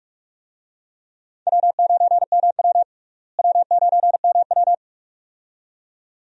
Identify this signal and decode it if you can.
Morse code, heard as a keyed tone, about 42 wpm, 700 Hz. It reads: W9MW W9MW
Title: USA Call Signs - 4 Characters - Repeated 2x - 45wpm